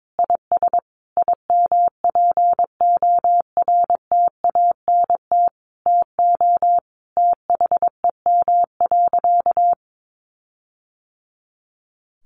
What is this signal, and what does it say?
Morse code, heard as a keyed tone, about 22 wpm, 700 Hz. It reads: IS IMPORTANT TO THEM.
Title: Sentences from Top 1400 Words 22wpm